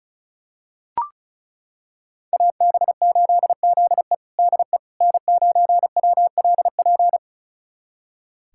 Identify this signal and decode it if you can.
Morse code, heard as a keyed tone, about 34 wpm, 700 Hz. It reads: AB8ZE DE N9WLP
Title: USA Callsign DE Callsign - 5 Characters - 35wpm